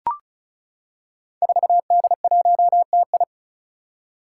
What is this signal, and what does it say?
Morse code, heard as a keyed tone, about 36 wpm, 700 Hz. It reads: 4D1TI
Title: Five Challenging Letter-Number Combinations 35wpm